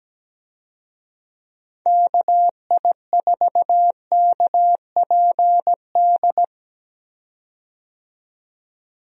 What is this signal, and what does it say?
Morse code, heard as a keyed tone, about 17 wpm, 700 Hz. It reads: KI4KPD
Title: USA Call Signs - 6 Characters - 17wpm